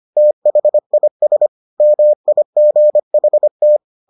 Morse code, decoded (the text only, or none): THIS MIGHT